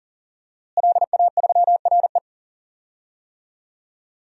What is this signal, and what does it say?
Morse code, heard as a keyed tone, about 40 wpm, 700 Hz. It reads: LA3RE
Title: DXCC-100 Call Signs - 5 Characters - Repeated 2x - 40wpm